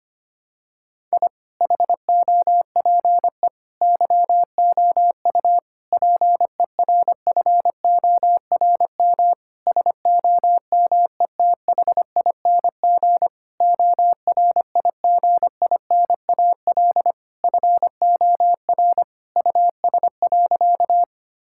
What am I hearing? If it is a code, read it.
Morse code, heard as a keyed tone, about 25 wpm, 700 Hz. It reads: I HOPE YOU PERFORM SOMETHING ORIGINAL FOR US.
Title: Sentences from Top 1000 Words - Review Part 6 - 25wpm